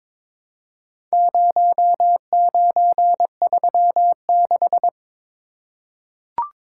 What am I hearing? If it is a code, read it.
Morse code, heard as a keyed tone, about 22 wpm, 700 Hz. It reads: 0936
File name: practice / 22wpm / Four Letter-Number Combinations - 22wpm